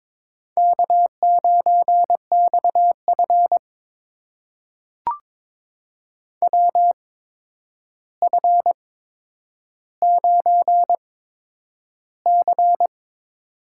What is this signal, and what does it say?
Morse code, heard as a keyed tone, about 22 wpm, spaced as for 9 wpm, 700 Hz. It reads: K9XF WF9C
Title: USA Call Signs - 4 Characters - 8x Character Spacing - 22wpm